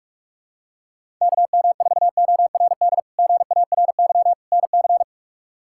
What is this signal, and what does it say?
Morse code, heard as a keyed tone, about 46 wpm, 700 Hz. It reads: KM4ORD GARY NC
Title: K1USN SST QSO Practice - 45wpm